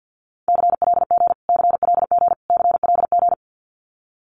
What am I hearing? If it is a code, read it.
Morse code, heard as a keyed tone, about 50 wpm, 700 Hz. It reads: CLG CLG CLG
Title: Abbreviations Repeated 3x - 50wpm